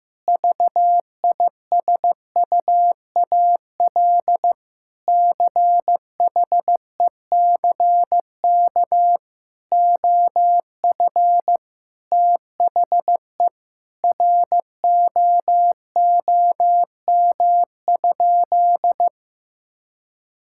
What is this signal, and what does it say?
Morse code, heard as a keyed tone, about 15 wpm, 700 Hz. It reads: VISUAL CHECK OF THE ROOM?
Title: Sentences from Top 1000 Words - Review Part 6 - 15wpm